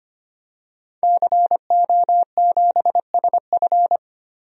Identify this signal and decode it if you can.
Morse code, heard as a keyed tone, about 25 wpm, 700 Hz. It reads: CO7SF
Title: DXCC-100 Call Signs - 5 Characters - Repeated 2x - 25wpm